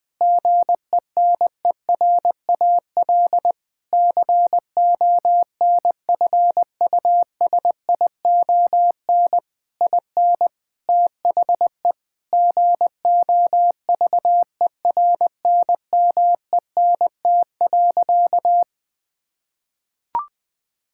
Morse code, heard as a keyed tone, about 20 wpm, 700 Hz. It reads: GENERAL CONFUSION IN THE GOVERNMENT.
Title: Sentences from Top 1600 Words 20wpm